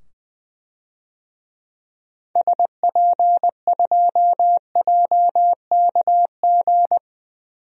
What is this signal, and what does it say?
Morse code, heard as a keyed tone, about 20 wpm, 700 Hz. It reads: SP2JKG
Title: DXCC-100 Call Signs - 6 Characters - 20wpm